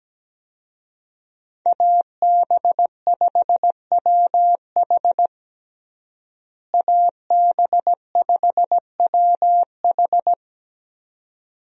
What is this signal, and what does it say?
Morse code, heard as a keyed tone, about 17 wpm, 700 Hz. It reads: AB5WH AB5WH
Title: USA Call Signs - 5 Characters - Repeated 2x - 17wpm